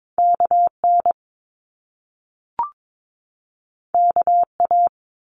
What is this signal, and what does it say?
Morse code, heard as a keyed tone, about 22 wpm, 700 Hz. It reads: KN KA
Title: Top Two Letter Combinations in Words 22wpm